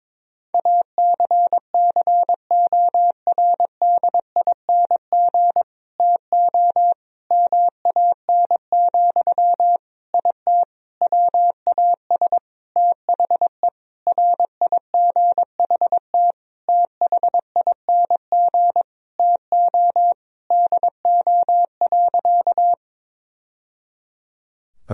Morse code, heard as a keyed tone, about 22 wpm, 700 Hz. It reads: ACCORDING TO MAN, IT WAS THE RIGHT THING TO DO.
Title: Sentences from Top 1000 Words - Review Part 4 - 22wpm